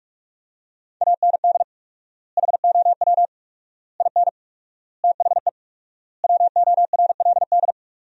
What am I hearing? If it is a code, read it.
Morse code, heard as a keyed tone, about 46 wpm, 700 Hz. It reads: AND HOW IN THE WORLD
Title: 4x Word-Spacing - Sentences from Top 200 Words - 45wpm